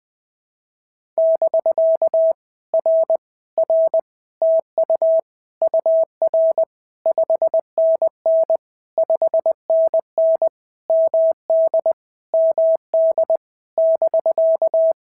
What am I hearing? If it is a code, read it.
Morse code, heard as a keyed tone, about 20 wpm, 650 Hz. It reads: <BK> R R TU UR 5NN 5NN MD MD <BK>
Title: POTA QSO Practice - 20wpm